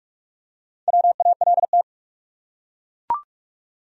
Morse code, heard as a keyed tone, about 42 wpm, 700 Hz. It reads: WALT
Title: USA Common Names - Max 4 Letters - 45wpm